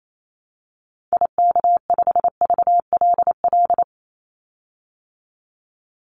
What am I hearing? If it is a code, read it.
Morse code, heard as a keyed tone, about 28 wpm, 700 Hz. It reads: IK5VLL
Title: Call Signs Repeated 2x - 28wpm